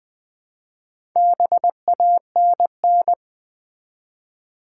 Morse code, heard as a keyed tone, about 20 wpm, 700 Hz. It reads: BANN
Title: Build A Word - 2 Letters At A Time - 6-Letter Words - 20wpm